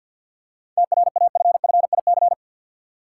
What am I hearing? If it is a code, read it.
Morse code, heard as a keyed tone, about 50 wpm, 700 Hz. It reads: TRAFFIC
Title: Seven Letter Words - 50wpm